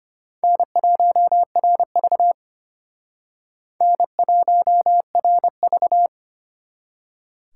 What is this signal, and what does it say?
Morse code, heard as a keyed tone, about 31 wpm, 700 Hz. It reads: N1RV N1RV
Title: Speed Racing - USA 1x2 Call Signs 25wpm